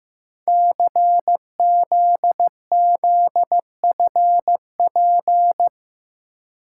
Morse code, heard as a keyed tone, about 15 wpm, 700 Hz. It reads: CZZFP
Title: Code Groups - 5 Letter:Numbers - 15wpm